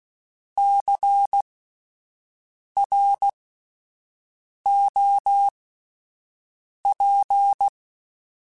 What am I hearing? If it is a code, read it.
Morse code, heard as a keyed tone, about 16 wpm, spaced as for 5 wpm, 800 Hz. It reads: CROP
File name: Spaced Alphas / Spaced Alphas mp3s / Cs sp @10